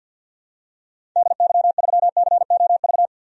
Morse code, heard as a keyed tone, about 53 wpm, 700 Hz. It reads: DY3COV
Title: DXCC-100 Call Signs - 6 Characters - 50wpm